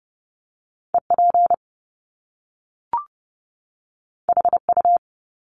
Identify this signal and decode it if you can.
Morse code, heard as a keyed tone, about 30 wpm, 700 Hz. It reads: EP HU
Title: Top Two Letter Combinations in Words 30wpm